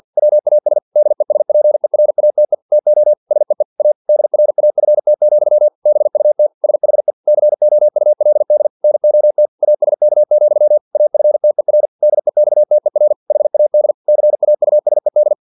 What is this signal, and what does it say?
Morse code, heard as a keyed tone, about 50 wpm, 600 Hz. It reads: WAS DESPERATE TO SEE A DRAFT, BUT SHE COULD NOT ASK, AFTER DEXTER HAD CAUSED
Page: Never 49wpm